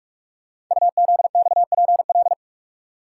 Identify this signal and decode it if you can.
Morse code, heard as a keyed tone, about 44 wpm, 700 Hz. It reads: UZXPL
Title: Code Groups - 5 Letters - 45wpm